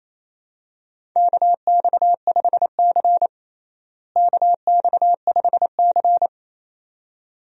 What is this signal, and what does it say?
Morse code, heard as a keyed tone, about 27 wpm, 700 Hz. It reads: KX5C KX5C
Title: USA Call Signs - 4 Characters - Repeated 2x - 28wpm